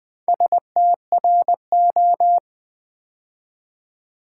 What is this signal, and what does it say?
Morse code, heard as a keyed tone, about 20 wpm, 700 Hz. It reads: STRO
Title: Build A Word - 2 Letters At A Time - 6-Letter Words - 20wpm